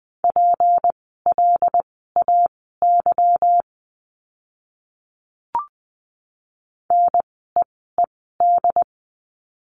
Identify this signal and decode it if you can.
Morse code, heard as a keyed tone, about 20 wpm, spaced as for 15 wpm, 700 Hz. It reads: PLAY NEED
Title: 237 Lesson 38f - Letter q - Words 20wpm